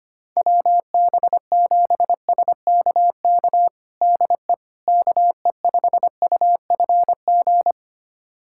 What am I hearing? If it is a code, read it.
Morse code, heard as a keyed tone, about 25 wpm, 700 Hz. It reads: WB7SKK DE KE5UFG
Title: USA Callsign DE Callsign - 6 Characters - 25wpm